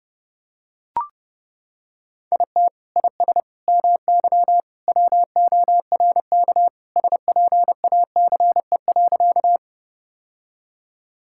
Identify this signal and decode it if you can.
Morse code, heard as a keyed tone, about 30 wpm, 700 Hz. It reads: IT IS MY WORK SPACE.
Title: Sentences from Top 1000 Words - Review Part 5 - 30wpm